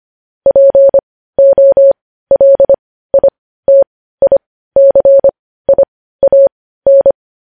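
Morse code, heard as a keyed tone, about 25 wpm, spaced as for 15 wpm, 550 Hz. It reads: POLITICIAN
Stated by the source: Morse news headlines sent at 15 words per minute.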